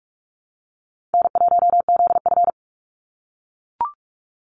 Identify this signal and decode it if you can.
Morse code, heard as a keyed tone, about 44 wpm, 700 Hz. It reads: N1ZF
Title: NA SOTA Activator Callsigns 45wpm